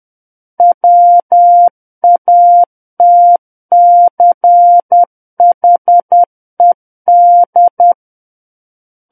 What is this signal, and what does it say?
Morse code, heard as a keyed tone, about 10 wpm, 700 Hz. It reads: WATCHED
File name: Book One / 01-Chapter 1 - 10 WPM